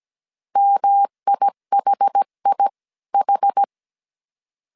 Morse code, heard as a keyed tone, about 17 wpm, spaced as for 15 wpm, 800 Hz. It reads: MIHIH